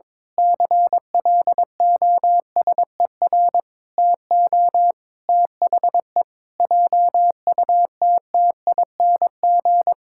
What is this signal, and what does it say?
Morse code, heard as a keyed tone, about 21 wpm, 700 Hz. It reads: CLOSER TO THE JUTTING
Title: Pangram Challenge v2 - 22wpm